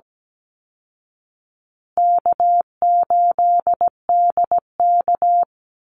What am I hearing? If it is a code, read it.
Morse code, heard as a keyed tone, about 17 wpm, 700 Hz. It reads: K8DK
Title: Easy Call Signs Repeated 3x - 17wpm